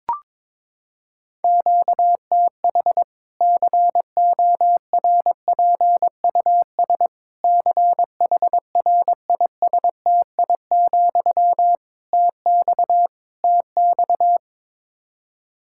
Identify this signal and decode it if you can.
Morse code, heard as a keyed tone, about 22 wpm, 700 Hz. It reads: QTH CORPUS CHRISTI, TX TX
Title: K1USN SST QSO Practice 22wpm